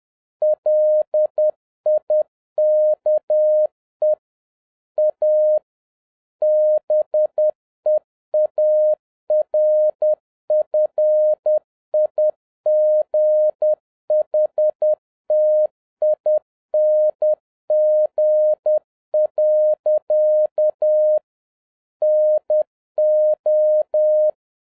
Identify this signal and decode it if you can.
Morse code, heard as a keyed tone, about 10 wpm, 610 Hz. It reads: LIKE A BEAR FIGHTING. NO
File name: 2011-07-mp3 / n56 BACK FROM A BIG ACCIDENT 10wpm